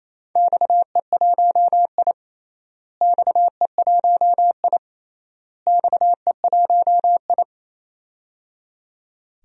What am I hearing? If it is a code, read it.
Morse code, heard as a keyed tone, about 28 wpm, 700 Hz. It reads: XE1I XE1I XE1I
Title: DXCC-100 Call Signs - 4 Characters - Repeated 3x - 28wpm